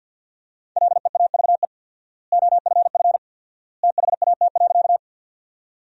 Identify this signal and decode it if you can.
Morse code, heard as a keyed tone, about 52 wpm, 700 Hz. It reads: LEAVE OFF THAT.
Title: 4x Word-Spacing - Sentences from Top 400 Words - 50wpm